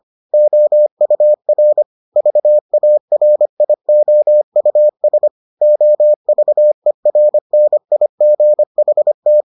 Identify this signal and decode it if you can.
Morse code, heard as a keyed tone, about 25 wpm, 600 Hz. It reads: OUR VARIOUS OVERNIGHT